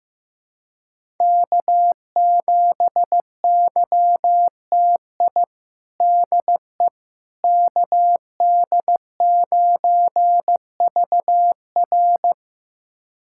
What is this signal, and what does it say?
Morse code, heard as a keyed tone, about 16 wpm, 700 Hz. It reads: K7YTI DE KD9VR
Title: USA Callsign DE Callsign - 5 Characters - 15wpm